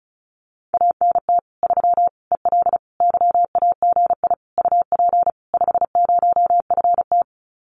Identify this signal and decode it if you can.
Morse code, heard as a keyed tone, about 34 wpm, 700 Hz. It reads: ANT 3 EL YAGI UP 50FT
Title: Common QSO Phrases Encore 35wpm